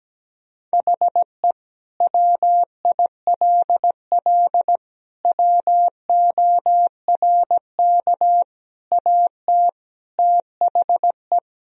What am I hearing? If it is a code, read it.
Morse code, heard as a keyed tone, about 17 wpm, 700 Hz. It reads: HE WILL WORK AT THE
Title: Sentences from Top 1000 Words - Review Part 1 - 17wpm